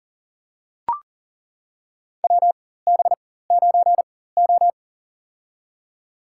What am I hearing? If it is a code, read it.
Morse code, heard as a keyed tone, about 40 wpm, spaced as for 19 wpm, 700 Hz. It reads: WB9O
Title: USA Call Signs - 4 Characters - 4x Character Spacing - 40wpm